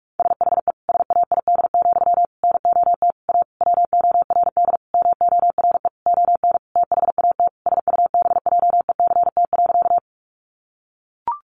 Morse code, heard as a keyed tone, about 44 wpm, 700 Hz. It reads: SHE SAID, NOT A WORD MORE ON THAT SUBJECT.
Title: Sentences from Top 800 Words 45wpm